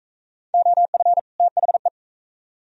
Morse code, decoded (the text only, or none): OF THE